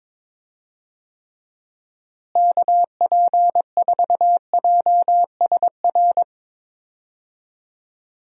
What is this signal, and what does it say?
Morse code, heard as a keyed tone, about 21 wpm, 700 Hz. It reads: KP4JSR
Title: USA Call Signs - 6 Characters - Repeated 2x - 22wpm